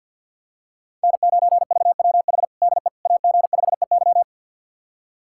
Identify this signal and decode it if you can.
Morse code, heard as a keyed tone, about 50 wpm, 700 Hz. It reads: N9VWH DE AG5EY
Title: USA Callsign DE Callsign - 5 Characters - 50wpm